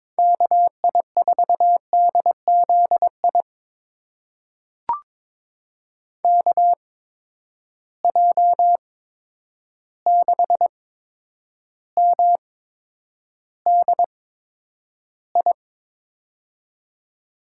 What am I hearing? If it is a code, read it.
Morse code, heard as a keyed tone, about 22 wpm, spaced as for 9 wpm, 700 Hz. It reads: KI4DZI KJ6MDI
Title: USA Call Signs - 6 Characters - 8x Character Spacing - 22wpm